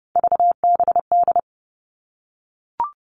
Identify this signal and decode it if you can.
Morse code, heard as a keyed tone, about 30 wpm, 700 Hz. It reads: VBD
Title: All Cut Numbers - 3-Digit 30wpm